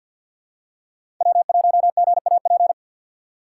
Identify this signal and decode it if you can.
Morse code, heard as a keyed tone, about 50 wpm, 700 Hz. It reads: W1GAP
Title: USA Call Signs - 2x Character Spacing - 50wpm